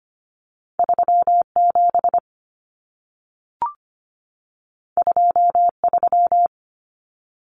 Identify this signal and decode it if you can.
Morse code, heard as a keyed tone, about 25 wpm, 700 Hz. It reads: 37 23
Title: Numbers - 2-Digit 25wpm